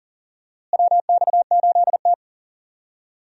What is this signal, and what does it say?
Morse code, heard as a keyed tone, about 40 wpm, 700 Hz. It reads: WX8T
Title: USA Call Signs - 4 Characters - Repeated 3x - 40wpm